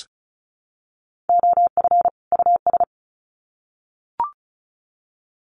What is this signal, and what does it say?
Morse code, heard as a keyed tone, about 34 wpm, 700 Hz. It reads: OF US
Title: Sets of 2 Words 35wpm